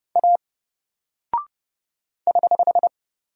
Morse code, heard as a keyed tone, about 29 wpm, 700 Hz. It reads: A <HH>
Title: Single Character (Letter:Number:Punctuation:Prosign) 30wpm